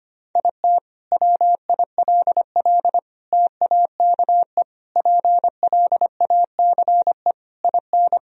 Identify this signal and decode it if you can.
Morse code, heard as a keyed tone, about 25 wpm, 700 Hz. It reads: IT WILL TAKE PLACE IN
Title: Sentences from Top 1000 Words - Review Part 2 - 25wpm